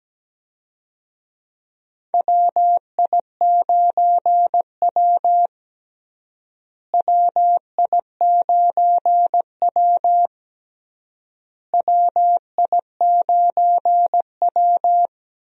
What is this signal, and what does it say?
Morse code, heard as a keyed tone, about 17 wpm, 700 Hz. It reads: WI9W WI9W WI9W
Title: USA Call Signs - 4 Characters - Repeated 3x - 17wpm